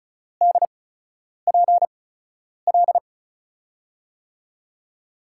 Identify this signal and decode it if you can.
Morse code, heard as a keyed tone, about 34 wpm, spaced as for 10 wpm, 700 Hz. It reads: DPL
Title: USA Call Signs - 5 Characters - 8x Character Spacing - 35wpm